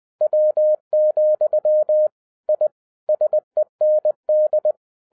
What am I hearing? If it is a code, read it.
Morse code, heard as a keyed tone, about 20 wpm, 610 Hz. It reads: W, I SEND
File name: 2011-10-mp3 / n13-TIPS FROM MY 34 YEARS OF EXPERIENCE 20wpm